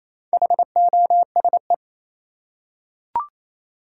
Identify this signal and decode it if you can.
Morse code, heard as a keyed tone, about 27 wpm, 700 Hz. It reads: HOSE
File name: Easy Words - Speed-Racing with Hints - 22wpm